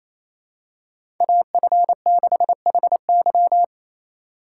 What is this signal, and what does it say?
Morse code, heard as a keyed tone, about 28 wpm, 700 Hz. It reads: AF6HY
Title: USA Call Signs - 5 Characters - 4x Character Spacing - 28wpm